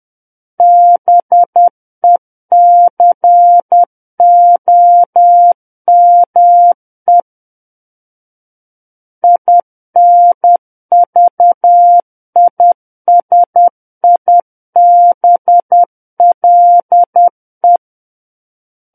Morse code, heard as a keyed tone, about 10 wpm, 700 Hz. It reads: BECOME INVISIBLE
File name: Book One / 01-Chapter 1 - 10 WPM